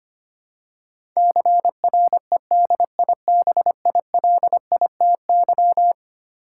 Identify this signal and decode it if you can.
Morse code, heard as a keyed tone, about 25 wpm, 700 Hz. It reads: CREDIBILITY
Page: morse-code-ninja-practice-25wpm